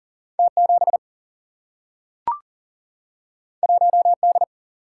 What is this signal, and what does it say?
Morse code, heard as a keyed tone, about 40 wpm, 700 Hz. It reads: T 7 1 D
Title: Single-Letter Single-Number 40wpm